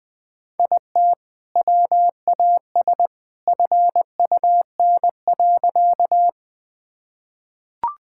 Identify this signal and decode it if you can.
Morse code, heard as a keyed tone, about 20 wpm, 700 Hz. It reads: IT WAS FUN.
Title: Sentences with 2-Word Phrases - Spoken Beginning - 20wpm